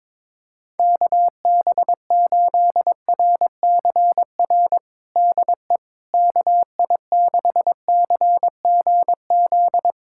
Morse code, heard as a keyed tone, about 22 wpm, 700 Hz. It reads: KB8RCR DE KI6CGZ
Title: USA Callsign DE Callsign - 6 Characters - 22wpm